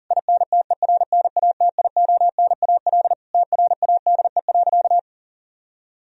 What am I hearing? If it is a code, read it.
Morse code, heard as a keyed tone, about 40 wpm, 700 Hz. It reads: INTERNATIONAL TRADE.
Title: Sentences from Top 1000 Words - Review Part 4 - 40wpm